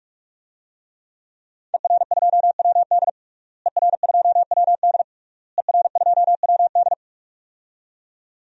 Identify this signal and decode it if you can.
Morse code, heard as a keyed tone, about 46 wpm, 700 Hz. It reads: ER2WD ER2WD ER2WD
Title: DXCC-100 Call Signs - 5 Characters - Repeated 3x - 45wpm